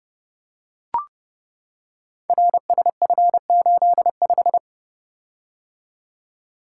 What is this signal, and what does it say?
Morse code, heard as a keyed tone, about 30 wpm, 700 Hz. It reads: RSF85
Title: Code Groups - 5 Letter:Numbers - 30wpm